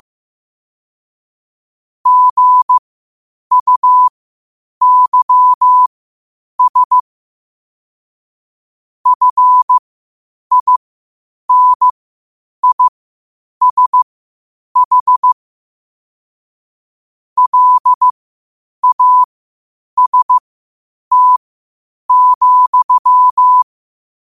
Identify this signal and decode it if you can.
Morse code, heard as a keyed tone, about 15 wpm, spaced as for 8 wpm, 1000 Hz. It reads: GUYS FINISH LAST,
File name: qotdfiles / 8wpm